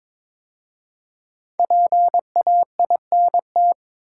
Morse code, heard as a keyed tone, about 22 wpm, 700 Hz. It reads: PAINT
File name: 5 Letter Words - First 3 Letters Given - 22wpm